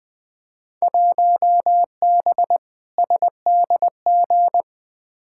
Morse code, heard as a keyed tone, about 21 wpm, 700 Hz. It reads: 1B SDG
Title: Field Day Exchanges Encore - 20wpm